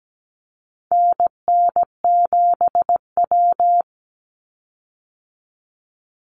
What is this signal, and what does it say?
Morse code, heard as a keyed tone, about 17 wpm, 700 Hz. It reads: NN7W
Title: Call Signs Repeated 3x - 17wpm